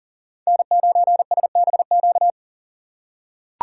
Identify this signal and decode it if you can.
Morse code, heard as a keyed tone, about 38 wpm, 690 Hz. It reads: N9SBQ
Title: USA Call Signs - 4x Character Spacing - 40wpm